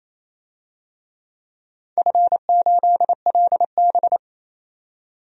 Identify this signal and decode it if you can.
Morse code, heard as a keyed tone, about 28 wpm, 700 Hz. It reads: F8LB
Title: DXCC-100 Call Signs - 4 Characters - 28wpm